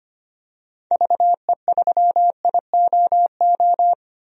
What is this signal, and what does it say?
Morse code, heard as a keyed tone, about 25 wpm, 700 Hz. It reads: VE3IOO
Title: DXCC-100 Call Signs - 6 Characters - 25wpm